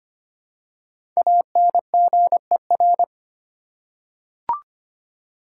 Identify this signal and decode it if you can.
Morse code, heard as a keyed tone, about 25 wpm, 700 Hz. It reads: ANGER
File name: 5 Letter Words - First 3 Letters Given - 25wpm